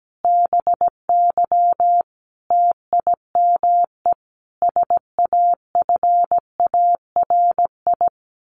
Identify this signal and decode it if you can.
Morse code, heard as a keyed tone, about 17 wpm, 700 Hz. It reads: BY TIME SAFARI
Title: A Sound Of Thunder 17wpm